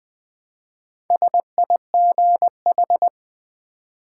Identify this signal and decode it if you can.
Morse code, heard as a keyed tone, about 20 wpm, 700 Hz. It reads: SIGH
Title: Build A Word - 1 Letter At A Time - 5-Letter Words - 20wpm